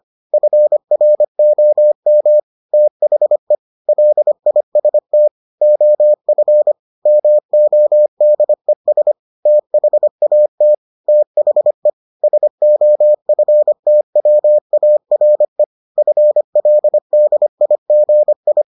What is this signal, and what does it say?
Morse code, heard as a keyed tone, about 25 wpm, 600 Hz. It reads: FROM THE LIST OF MODES THAT THE SOFTWARE FLDIGI